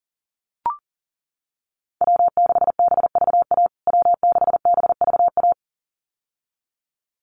Morse code, heard as a keyed tone, about 40 wpm, 700 Hz. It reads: W6BVA W6BVA
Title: USA Call Signs Repeated 2x 40wpm